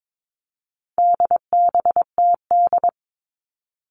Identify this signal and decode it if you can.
Morse code, heard as a keyed tone, about 22 wpm, 700 Hz. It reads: DBTD
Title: All Cut Numbers - 4-Digit 22wpm